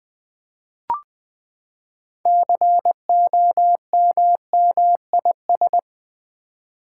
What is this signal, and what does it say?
Morse code, heard as a keyed tone, about 20 wpm, 700 Hz. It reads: COMMIS
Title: Build A Word - 6 Letters At A Time - 12-Letter Words - 20wpm